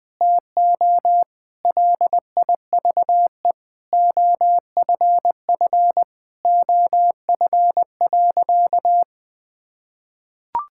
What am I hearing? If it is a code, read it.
Morse code, heard as a keyed tone, about 21 wpm, 700 Hz. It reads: TO LIVE OFF OF.
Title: Sentences from Top 1000 Words - Review Part 5 - 20wpm